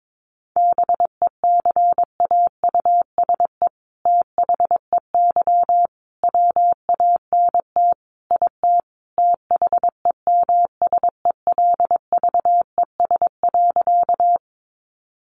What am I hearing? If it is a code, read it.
Morse code, heard as a keyed tone, about 22 wpm, 700 Hz. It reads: BECAUSE THEY WANT IT THEMSELVES.
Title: Sentences from Top 800 Words 22wpm